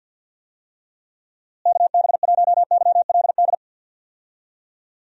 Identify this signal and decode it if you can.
Morse code, heard as a keyed tone, about 51 wpm, 700 Hz. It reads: KB1YLD
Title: USA Call Signs - 6 Characters - Repeated 2x - 50wpm